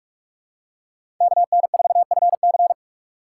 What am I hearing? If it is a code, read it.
Morse code, heard as a keyed tone, about 44 wpm, 700 Hz. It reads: KN4FC
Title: USA Call Signs - 5 Characters - 6x Character Spacing - 45wpm